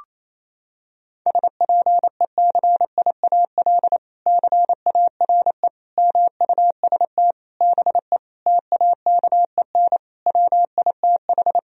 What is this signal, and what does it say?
Morse code, heard as a keyed tone, about 28 wpm, 700 Hz. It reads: SPECIAL CARE MUST BE TAKEN WITH
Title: Sentences from Top 1000 Words - Review Part 4 - 28wpm